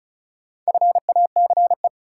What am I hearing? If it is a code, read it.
Morse code, heard as a keyed tone, about 36 wpm, 700 Hz. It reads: FACE
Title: Sets of 3 Words - New Word Drill - 35wpm